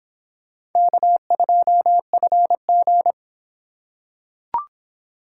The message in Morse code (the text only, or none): K2FG